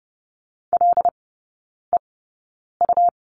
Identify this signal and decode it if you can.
Morse code, heard as a keyed tone, about 30 wpm, spaced as for 9 wpm, 700 Hz. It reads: LEU
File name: Lesson 17e - Letter u - Three Characters 30wpm